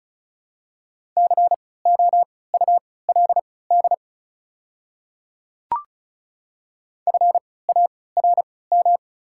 Morse code, heard as a keyed tone, about 34 wpm, spaced as for 20 wpm, 690 Hz. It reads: COULD FARM
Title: 3x Character Spacing - Top 500 Words - 35wpm